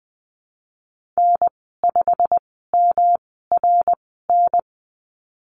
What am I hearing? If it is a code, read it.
Morse code, heard as a keyed tone, about 20 wpm, spaced as for 15 wpm, 700 Hz. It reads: N5MRN
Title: 147 Lesson 24g - Letter y - Call Signs 20wpm